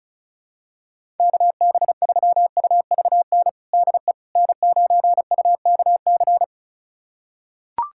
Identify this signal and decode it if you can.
Morse code, heard as a keyed tone, about 36 wpm, 690 Hz. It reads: KB3UVN DE N9UKC
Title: USA Callsign DE Callsign 35wpm